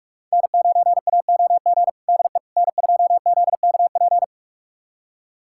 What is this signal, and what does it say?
Morse code, heard as a keyed tone, about 44 wpm, 700 Hz. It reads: N9AOG DE N2ZKP
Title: USA Callsign DE Callsign - 5 Characters - 45wpm